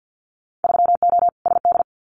Morse code, heard as a keyed tone, about 50 wpm, 700 Hz. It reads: 3O SD
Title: Winter Field Day Exchanges 50wpm